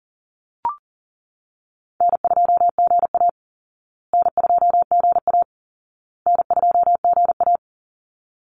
Morse code, heard as a keyed tone, about 40 wpm, 700 Hz. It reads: N2GA N2GA N2GA
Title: Easy Call Signs Repeated 3x - 40wpm